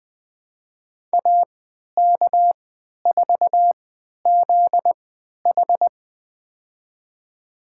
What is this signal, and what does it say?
Morse code, heard as a keyed tone, about 20 wpm, spaced as for 11 wpm, 700 Hz. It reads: AK4ZH
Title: USA Call Signs - 5 Characters - 3x Character Spacing - 20wpm